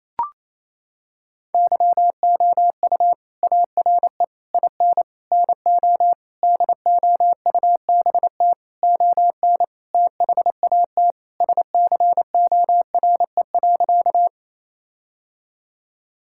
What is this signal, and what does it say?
Morse code, heard as a keyed tone, about 28 wpm, 700 Hz. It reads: YOU ARE IN NO DOUBT ON THAT SCORE.
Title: Sentences from Top 1600 Words 28wpm